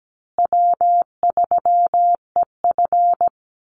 Morse code, heard as a keyed tone, about 17 wpm, 700 Hz. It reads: W3EF
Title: Easy Call Signs Repeated 3x - 17wpm